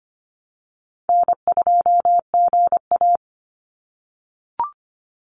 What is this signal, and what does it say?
Morse code, heard as a keyed tone, about 25 wpm, 700 Hz. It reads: N2GA
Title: Easy Call Signs - 25wpm